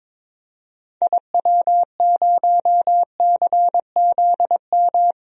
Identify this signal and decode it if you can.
Morse code, heard as a keyed tone, about 22 wpm, 700 Hz. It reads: IW0CZM
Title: DXCC-100 Call Signs - 6 Characters - 22wpm